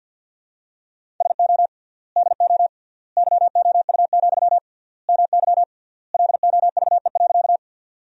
Morse code, heard as a keyed tone, about 50 wpm, 700 Hz. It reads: SO DO YOU, MY LOVE.
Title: 3x Word-Spacing - Sentences from Top 400 Words - 50wpm